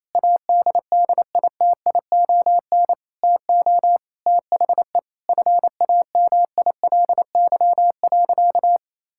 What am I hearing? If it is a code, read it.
Morse code, heard as a keyed tone, about 28 wpm, 700 Hz. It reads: ADDITION TO THE FAMILY.
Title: Sentences from Top 1000 Words - Review Part 6 - 28wpm